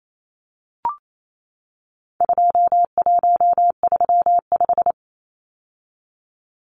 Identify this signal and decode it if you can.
Morse code, heard as a keyed tone, about 28 wpm, 700 Hz. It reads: 2135
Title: Numbers - 4-Digit 28wpm